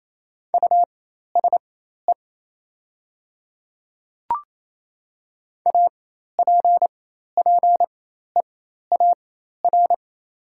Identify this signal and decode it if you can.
Morse code, heard as a keyed tone, about 28 wpm, spaced as for 13 wpm, 700 Hz. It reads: USE APPEAR
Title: 4x Character Spacing - Top 500 Words - 28wpm